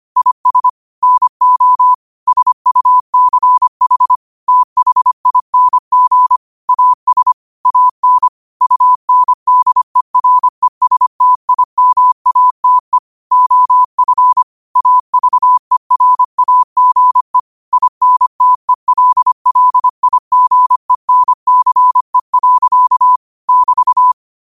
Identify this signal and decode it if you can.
Morse code, heard as a keyed tone, about 25 wpm, 1000 Hz. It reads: IS NO SUCH THING AS AN UNDERESTIMATE OF AVERAGE INTELLIGENCE. <BT>
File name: qotdfiles / 25wpm